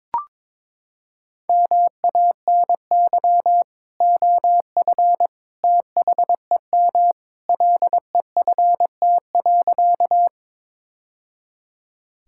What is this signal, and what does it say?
Morse code, heard as a keyed tone, about 22 wpm, 700 Hz. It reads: MANY OF THEM LEFT.
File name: practice / 22wpm / Sentences with 2-Word Phrases - 22wpm